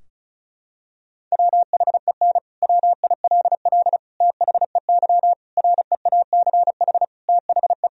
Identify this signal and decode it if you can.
Morse code, heard as a keyed tone, about 34 wpm, 700 Hz. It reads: WHEN WILL THEY REACH THE
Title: Sentences from Top 1000 Words - Review Part 4 - 35wpm